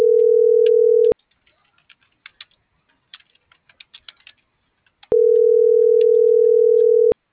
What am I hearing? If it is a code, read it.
Ringback tone (North America), 440 + 480 Hz.